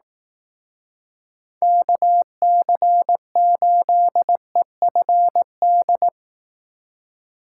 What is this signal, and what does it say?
Morse code, heard as a keyed tone, about 18 wpm, 700 Hz. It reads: KC8EFD
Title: Speed Racing - USA Call Signs 15wpm